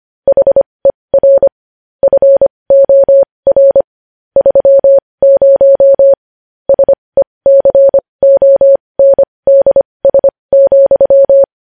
Morse code, heard as a keyed tone, about 25 wpm, spaced as for 20 wpm, 550 Hz. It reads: HER FOR 30 SECONDS,
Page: news-headlines-in-morse-code-at-20-wpm